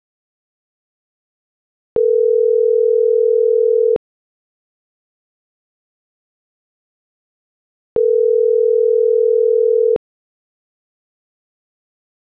Ringback tone (North America), 440 + 480 Hz.